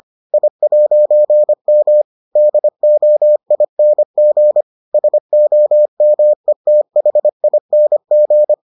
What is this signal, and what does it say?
Morse code, heard as a keyed tone, about 25 wpm, 600 Hz. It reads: I'M DOING SOMETHING